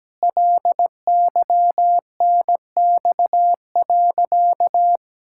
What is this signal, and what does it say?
Morse code, heard as a keyed tone, about 17 wpm, 700 Hz. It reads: LYNX.
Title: Pangram Challenge v2 - 17wpm